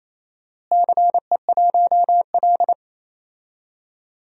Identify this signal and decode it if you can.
Morse code, heard as a keyed tone, about 28 wpm, 700 Hz. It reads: CE1L
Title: DXCC-100 Call Signs - 4 Characters - Repeated 3x - 28wpm